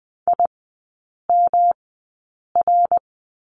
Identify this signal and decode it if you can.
Morse code, heard as a keyed tone, about 19 wpm, 700 Hz. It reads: I M R
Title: 121 Lesson 20e - Number 3 - Three Characters 20wpm